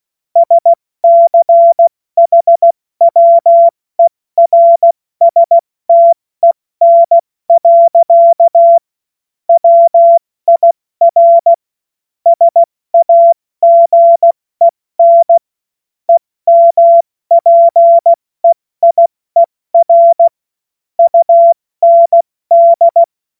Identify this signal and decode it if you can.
Morse code, heard as a keyed tone, about 16 wpm, 680 Hz. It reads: SCHWERSTEN. WIR SAGEN EMPEIER UND
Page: cw-witze